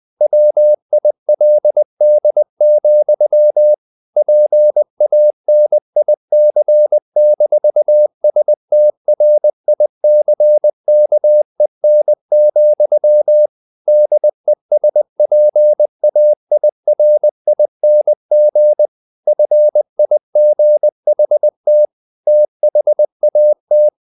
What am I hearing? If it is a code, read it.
Morse code, heard as a keyed tone, about 20 wpm, 600 Hz. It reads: WILD, PANIC-STRICKEN, DESPAIRING FIGHT THAT
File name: the old man and the sea / the old man and the sea part 1 20wpm